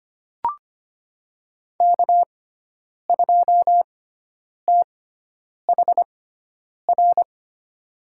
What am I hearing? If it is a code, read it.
Morse code, heard as a keyed tone, about 25 wpm, spaced as for 9 wpm, 700 Hz. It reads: K2THR
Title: USA Call Signs - 5 Characters - 6x Character Spacing - 25wpm